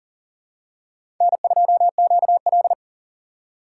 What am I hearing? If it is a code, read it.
Morse code, heard as a keyed tone, about 40 wpm, 700 Hz. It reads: N2QL
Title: USA Call Signs - 4 Characters - 4x Character Spacing - 40wpm